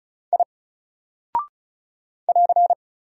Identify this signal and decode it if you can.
Morse code, heard as a keyed tone, about 36 wpm, 700 Hz. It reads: I <AR>
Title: Single Character (Letter:Number:Punctuation:Prosign) Alt 35wpm